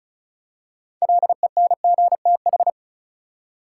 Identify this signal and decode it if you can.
Morse code, heard as a keyed tone, about 36 wpm, 700 Hz. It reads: LENGTH
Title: Six Letter Words - 35wpm